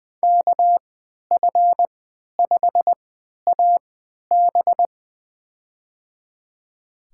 Morse code, heard as a keyed tone, about 20 wpm, spaced as for 11 wpm, 700 Hz. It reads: KF5AB
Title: USA Call Signs - 5 Characters - 3x Character Spacing - 20wpm